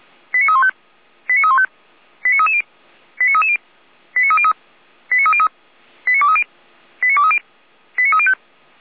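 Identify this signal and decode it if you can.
Five-tone selective call (ZVEI), heard as quick runs of five tones, tones of about 70 ms. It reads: 89315 89315 89330 89330 89303 89303 89230 89230 89305
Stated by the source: ZVEI five-tone selective call 89320 received on SDR.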